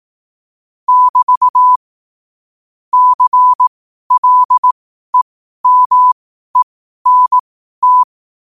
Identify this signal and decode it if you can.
Morse code, heard as a keyed tone, about 18 wpm, spaced as for 12 wpm, 1000 Hz. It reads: <BT> CLEMENT
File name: qotdfiles / 12wpm